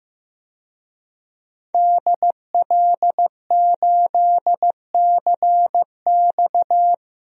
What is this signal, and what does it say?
Morse code, heard as a keyed tone, about 15 wpm, 700 Hz. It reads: DL8CX
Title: DXCC-100 Call Signs - 5 Characters - 15wpm